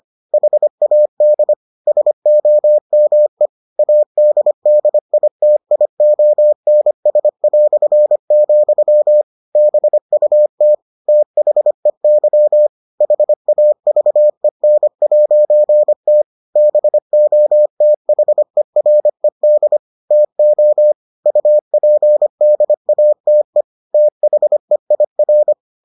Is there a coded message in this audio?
Yes: Morse code from 0.3 to 25.5 s, heard as a keyed tone, about 25 wpm, 600 Hz. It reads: HAD SOME ADDITIONS", BUT THEY HAVEN'T BOTHERED TO UPDATE THEIR